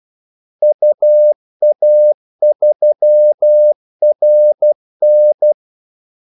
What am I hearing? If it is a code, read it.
Morse code, heard as a keyed tone, about 12 wpm, 600 Hz. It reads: UA3RN
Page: 2 ekstra ordmellemrum mellem kaldesignalerne